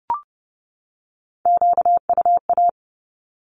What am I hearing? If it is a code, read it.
Morse code, heard as a keyed tone, about 30 wpm, 700 Hz. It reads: QUA
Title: Q-Codes 30wpm